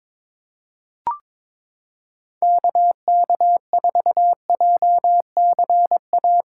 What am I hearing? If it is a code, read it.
Morse code, heard as a keyed tone, about 21 wpm, 700 Hz. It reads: KK4JCA
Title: USA Callsign DE Callsign - 6 Characters - 22wpm